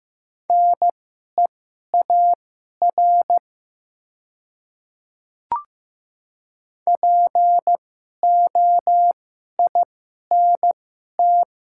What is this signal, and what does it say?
Morse code, heard as a keyed tone, about 15 wpm, spaced as for 11 wpm, 700 Hz. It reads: NEAR POINT
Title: 2x Character Spacing - Top 200 Words - 15wpm